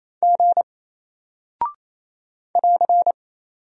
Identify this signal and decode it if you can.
Morse code, heard as a keyed tone, about 28 wpm, 700 Hz. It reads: G <AR>
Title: Single Character (Letter:Number:Punctuation:Prosign) 28wpm